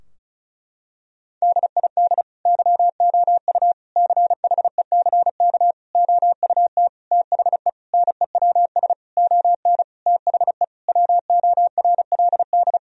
Morse code, heard as a keyed tone, about 36 wpm, 700 Hz. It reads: DID YOU CHECK OUT THE NEWS ON THE WORLD
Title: Sentences from Top 1000 Words - Review Part 4 - 35wpm